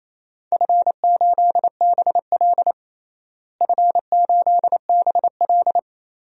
Morse code, heard as a keyed tone, about 28 wpm, 700 Hz. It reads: F8BL F8BL
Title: DXCC-100 Call Signs - 4 Characters - Repeated 2x - 28wpm